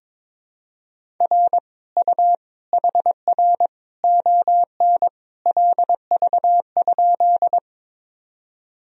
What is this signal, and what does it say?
Morse code, heard as a keyed tone, about 22 wpm, 700 Hz. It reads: R U HR ON LV?
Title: Sentences from Top 400 Words and Abbreviations 22wpm